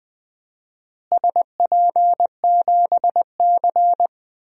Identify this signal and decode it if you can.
Morse code, heard as a keyed tone, about 20 wpm, 700 Hz. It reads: SP7C
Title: DXCC-100 Call Signs - 4 Characters - Repeated 3x - 20wpm